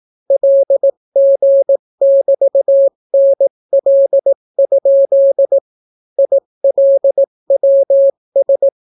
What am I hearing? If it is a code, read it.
Morse code, heard as a keyed tone, about 18 wpm, 550 Hz. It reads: LG<BT>NL? ILWS